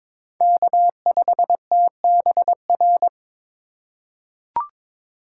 Morse code, heard as a keyed tone, about 22 wpm, 700 Hz. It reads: K5TBR
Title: USA Call Signs - 5 Characters - 3x Character Spacing - 22wpm